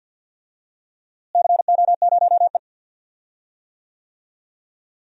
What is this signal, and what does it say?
Morse code, heard as a keyed tone, about 47 wpm, 700 Hz. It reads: CO0E
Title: DXCC-100 Call Signs - 4 Characters - Repeated 3x - 50wpm